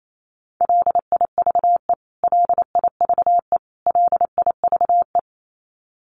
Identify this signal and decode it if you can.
Morse code, heard as a keyed tone, about 28 wpm, 700 Hz. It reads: LIVE LIVE LIVE
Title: Top 500 Words Repeated 28wpm